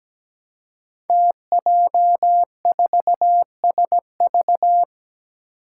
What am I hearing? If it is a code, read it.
Morse code, heard as a keyed tone, about 17 wpm, 700 Hz. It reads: TJ4SV
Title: Five Challenging Letter-Number Combinations 17wpm